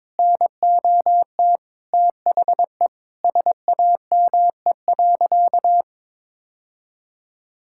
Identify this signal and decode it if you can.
Morse code, heard as a keyed tone, about 22 wpm, 700 Hz. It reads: NOT THE SAME.
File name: practice / 22wpm / Sentences with 2-Word Phrases - 22wpm